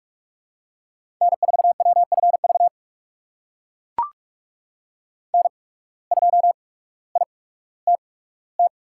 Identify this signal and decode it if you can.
Morse code, heard as a keyed tone, about 44 wpm, spaced as for 19 wpm, 690 Hz. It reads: N4WFV N2ITT
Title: USA Call Signs - 5 Characters - 8x Character Spacing - 45wpm